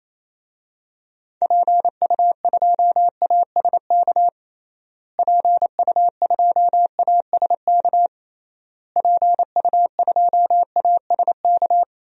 Morse code, heard as a keyed tone, about 28 wpm, 700 Hz. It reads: PU2ASK PU2ASK PU2ASK
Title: DXCC-100 Call Signs - 6 Characters - Repeated 3x - 28wpm